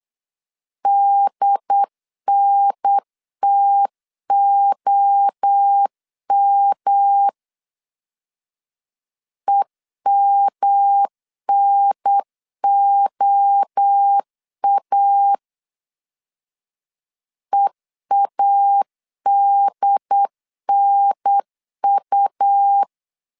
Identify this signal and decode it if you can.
Morse code, heard as a keyed tone, about 8 wpm, 800 Hz. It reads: DNTOM EMNOA EADNU